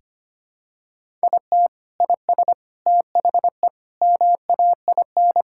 Morse code, heard as a keyed tone, about 25 wpm, 700 Hz. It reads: IT IS THE MAIN
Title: Sentences from Top 1000 Words - Review Part 5 - 25wpm